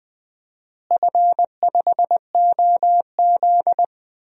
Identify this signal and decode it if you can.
Morse code, heard as a keyed tone, about 20 wpm, 700 Hz. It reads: F5OZ
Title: DXCC-100 Call Signs - 4 Characters - Repeated 2x - 20wpm